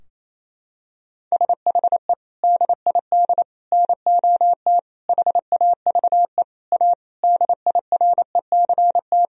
Morse code, heard as a keyed tone, about 27 wpm, 700 Hz. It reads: SHE DID NOT HAVE A DIRECT
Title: Sentences from Top 1000 Words - Review Part 2 - 28wpm